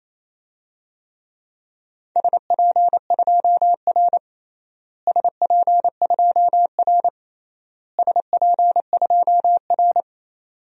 Morse code, heard as a keyed tone, about 28 wpm, 700 Hz. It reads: SP2R SP2R SP2R
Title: DXCC-100 Call Signs - 4 Characters - Repeated 3x - 28wpm